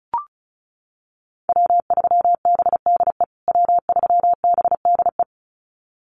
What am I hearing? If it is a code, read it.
Morse code, heard as a keyed tone, about 34 wpm, 700 Hz. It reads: W3BDE W3BDE
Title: USA Call Signs Repeated 2x 35wpm